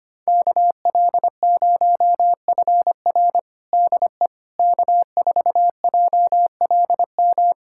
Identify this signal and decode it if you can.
Morse code, heard as a keyed tone, about 25 wpm, 700 Hz. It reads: KL0FR DE K4JLM
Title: USA Callsign DE Callsign - 5 Characters - 25wpm